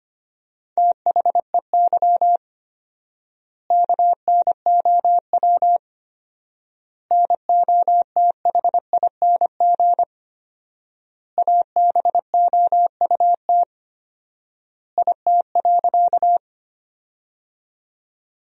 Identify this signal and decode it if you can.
Morse code, heard as a keyed tone, about 25 wpm, 700 Hz. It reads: THEY KNOW NOTHING ABOUT IT.
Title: 4x Word-Spacing - Sentences from Top 500 Words - 25wpm